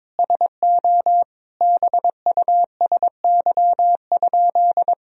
Morse code, heard as a keyed tone, about 22 wpm, 700 Hz. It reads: SO BUSY?
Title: Sentences from Top 1000 Words - Review Part 3 - 22wpm